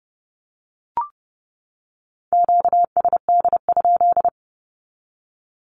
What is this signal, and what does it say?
Morse code, heard as a keyed tone, about 29 wpm, 700 Hz. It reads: QSD?
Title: Q-Codes 30wpm